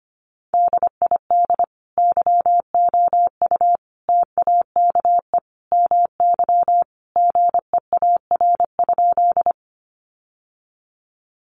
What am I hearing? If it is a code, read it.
Morse code, heard as a keyed tone, about 25 wpm, 700 Hz. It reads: DID YOU TAKE MY GEAR?
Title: Sentences from Top 1400 Words 25wpm